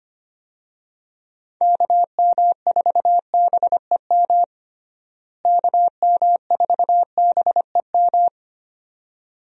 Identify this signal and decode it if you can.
Morse code, heard as a keyed tone, about 25 wpm, 700 Hz. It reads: KM4BEM KM4BEM
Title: USA Call Signs - 6 Characters - Repeated 2x - 25wpm